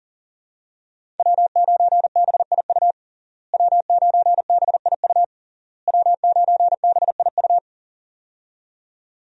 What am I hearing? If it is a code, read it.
Morse code, heard as a keyed tone, about 37 wpm, 700 Hz. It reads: W9BIU W9BIU W9BIU
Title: USA Call Signs - 5 Characters - Repeated 3x - 40wpm